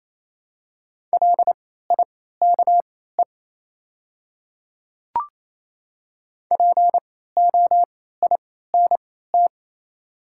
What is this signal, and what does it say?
Morse code, heard as a keyed tone, about 28 wpm, spaced as for 16 wpm, 700 Hz. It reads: LIKE POINT
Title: 3x Character Spacing - Top 200 Words - 28wpm